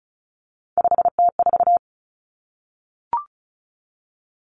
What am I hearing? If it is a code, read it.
Morse code, heard as a keyed tone, about 34 wpm, 700 Hz. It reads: 5T4
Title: Common Cut Numbers - 3-Digit 35wpm